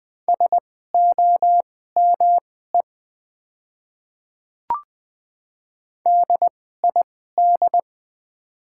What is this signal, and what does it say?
Morse code, heard as a keyed tone, about 20 wpm, spaced as for 14 wpm, 700 Hz. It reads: SOME DID
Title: 2x Character Spacing - Top 100 Words - 20wpm